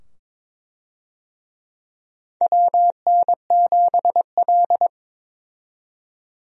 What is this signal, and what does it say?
Morse code, heard as a keyed tone, about 22 wpm, 700 Hz. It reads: WN7L
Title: USA Call Signs - 4 Characters - 22wpm